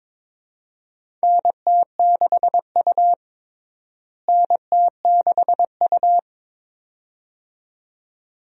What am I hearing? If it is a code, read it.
Morse code, heard as a keyed tone, about 22 wpm, 700 Hz. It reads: NT6U NT6U
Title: USA Call Signs - 4 Characters - Repeated 2x - 22wpm